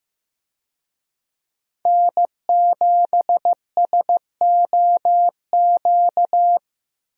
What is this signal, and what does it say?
Morse code, heard as a keyed tone, about 15 wpm, 700 Hz. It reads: N7SOQ
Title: USA Call Signs - 5 Characters - 15wpm